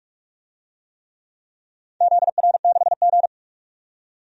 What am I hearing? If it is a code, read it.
Morse code, heard as a keyed tone, about 41 wpm, 700 Hz. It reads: ZR6G
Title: DXCC-100 Call Signs - 4 Characters - 45wpm